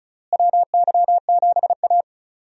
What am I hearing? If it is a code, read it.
Morse code, heard as a keyed tone, about 36 wpm, 700 Hz. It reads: WY7A
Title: USA Call Signs - 4 Characters - Repeated 3x - 35wpm